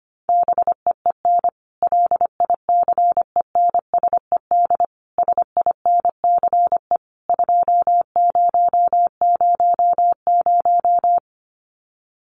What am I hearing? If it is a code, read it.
Morse code, heard as a keyed tone, about 25 wpm, 700 Hz. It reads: BEEN LICENSED SINCE 2000
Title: Common QSO Phrases 25wpm